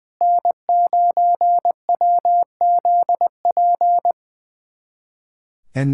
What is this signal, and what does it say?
Morse code, heard as a keyed tone, about 20 wpm, 700 Hz. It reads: N9WZP
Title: USA Callsign DE Callsign - 5 Characters - 20wpm